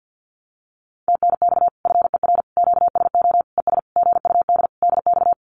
Morse code, heard as a keyed tone, about 50 wpm, 700 Hz. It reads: TNX FER QSO ES GUD DX
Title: Common QSO Phrases Encore 50wpm